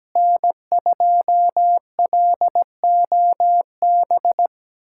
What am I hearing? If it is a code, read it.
Morse code, heard as a keyed tone, about 17 wpm, 700 Hz. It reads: N2LOB
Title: USA Callsign DE Callsign - 5 Characters - 17wpm